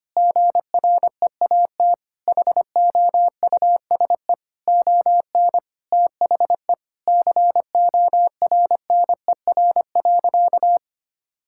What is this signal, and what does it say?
Morse code, heard as a keyed tone, about 25 wpm, 700 Hz. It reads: GREAT HOUSE ON THE CORNER.
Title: Sentences from Top 1600 Words 25wpm